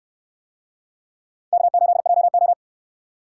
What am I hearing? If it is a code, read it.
Morse code, heard as a keyed tone, about 69 wpm, 700 Hz. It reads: K8JO